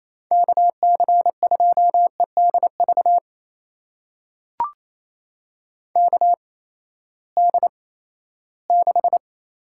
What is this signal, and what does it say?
Morse code, heard as a keyed tone, about 27 wpm, spaced as for 16 wpm, 700 Hz. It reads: KC2EDV KD6
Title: USA Call Signs - 6 Characters - 8x Character Spacing - 28wpm